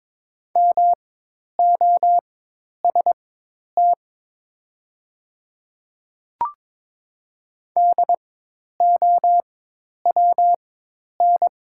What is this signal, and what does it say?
Morse code, heard as a keyed tone, about 22 wpm, spaced as for 10 wpm, 700 Hz. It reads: MOST DOWN
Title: 4x Character Spacing - Top 200 Words - 22wpm